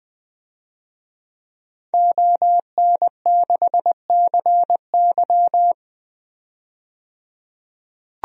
Morse code, heard as a keyed tone, about 20 wpm, 700 Hz. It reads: ON6CY
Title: DXCC-100 Call Signs - 5 Characters - 20wpm